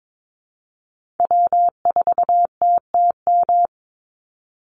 Morse code, heard as a keyed tone, about 22 wpm, 700 Hz. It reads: W4TTM
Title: Call Signs Repeated 3x - 22wpm